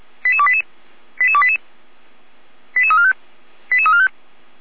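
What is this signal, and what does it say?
Five-tone selective call (ZVEI), heard as quick runs of five tones, tones of about 70 ms. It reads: 88288 88288 88345 88345
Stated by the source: ZVEI five-tone selective call 88289 received on SDR.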